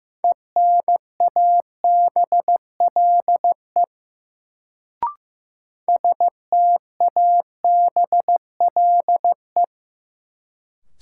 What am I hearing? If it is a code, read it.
Morse code, heard as a keyed tone, about 15 wpm, 700 Hz. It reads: ENABLE STABLE
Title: Sentences from Top 1500 Words - New Word Drill - 15wpm